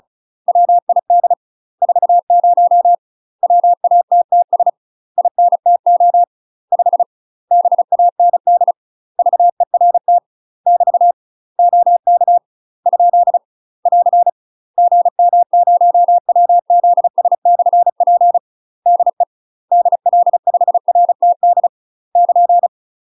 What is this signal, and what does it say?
Morse code, heard as a keyed tone, about 36 wpm, 700 Hz. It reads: WID 40 WATTS INTO 5 BAND VERT <BT> OK ? <AR> GM0WZS/P DE DL5RTD <KN>
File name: UK MASTER QSOs TRAINING / 35WPM / UK MASTER QSOs 0027